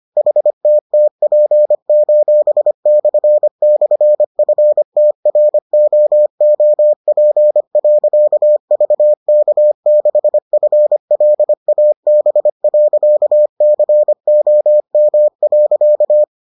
Morse code, heard as a keyed tone, about 25 wpm, 600 Hz. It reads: HTTP://FTROOP.VK6FLAB.COM.